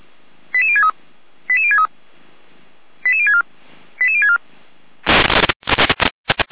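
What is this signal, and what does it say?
Five-tone selective call (ZVEI), heard as quick runs of five tones, tones of about 70 ms. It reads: 88063 88063 88064 88064
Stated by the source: ZVEI five-tone call received by SDR.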